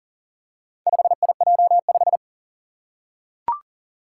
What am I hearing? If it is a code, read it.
Morse code, heard as a keyed tone, about 44 wpm, 700 Hz. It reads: 5IJ5
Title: Four Challenging Letter-Number Combinations 40wpm